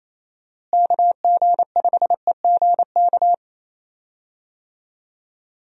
Morse code, heard as a keyed tone, about 28 wpm, 700 Hz. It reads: KG5EGK
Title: USA Call Signs - 6 Characters - Repeated 2x - 28wpm